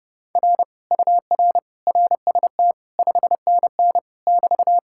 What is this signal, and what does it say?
Morse code, heard as a keyed tone, about 31 wpm, 700 Hz. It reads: R UR RST 5NN <BT>